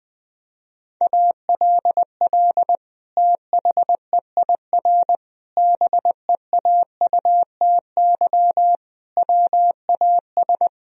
Morse code, heard as a keyed tone, about 20 wpm, 700 Hz. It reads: ALL THEIR BEAUTY WAS
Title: Sentences from Top 1000 Words - Review Part 2 - 20wpm